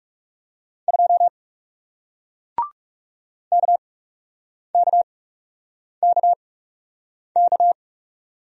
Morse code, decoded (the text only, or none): 2 KKKK